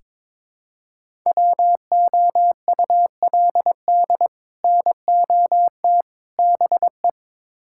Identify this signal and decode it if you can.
Morse code, heard as a keyed tone, about 21 wpm, 700 Hz. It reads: WOULD NOT BE
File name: practice / 22wpm / Build A Phrase - 5-Word Phrases - 22wpm